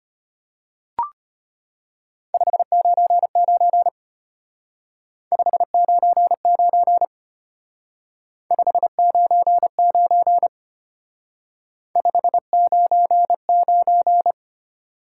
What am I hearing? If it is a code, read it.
Morse code, heard as a keyed tone, about 31 wpm, 700 Hz. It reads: 599 599 599 599
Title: Speed-Racing Top 90 QSO Elements 25wpm